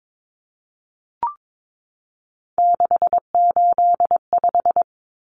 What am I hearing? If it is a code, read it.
Morse code, heard as a keyed tone, about 22 wpm, 700 Hz. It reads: 685
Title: Numbers - 3-Digit 22wpm